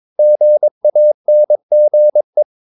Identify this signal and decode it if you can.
Morse code, heard as a keyed tone, about 22 wpm, 600 Hz. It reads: GANGE
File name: Ferie QRS 22wpm